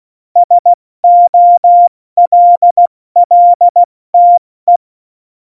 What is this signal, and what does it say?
Morse code, heard as a keyed tone, about 16 wpm, spaced as for 14 wpm, 700 Hz. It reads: SOLLTE
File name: Morsekurs / Witze / Witz 04. Juni 2019, 70 F80